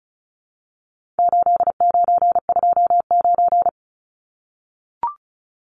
Morse code, heard as a keyed tone, about 34 wpm, 700 Hz. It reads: 8929
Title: Numbers - 4-Digit 35wpm